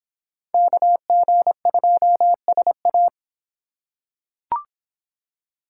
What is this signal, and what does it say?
Morse code, heard as a keyed tone, about 24 wpm, 700 Hz. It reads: KG2SA